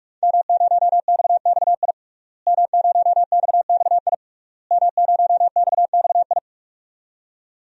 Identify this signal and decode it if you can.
Morse code, heard as a keyed tone, about 46 wpm, 700 Hz. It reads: M0XXI M0XXI M0XXI
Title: DXCC-100 Call Signs - 5 Characters - Repeated 3x - 45wpm